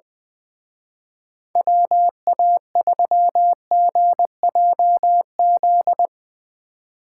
Morse code, heard as a keyed tone, about 22 wpm, 700 Hz. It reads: WA3GJZ
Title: Speed Racing - USA Call Signs 20wpm